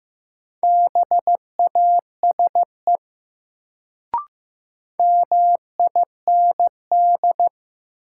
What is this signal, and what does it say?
Morse code, heard as a keyed tone, about 15 wpm, 700 Hz. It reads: BASE MIND
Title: Sets of 5 Words - New Word Drill - 15wpm